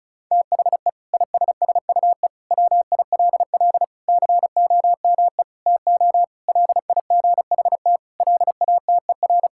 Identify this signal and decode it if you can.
Morse code, heard as a keyed tone, about 34 wpm, 700 Hz. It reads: THE ISSUE WILL COME TO LIGHT LATER
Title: Sentences from Top 1000 Words - Review Part 6 - 35wpm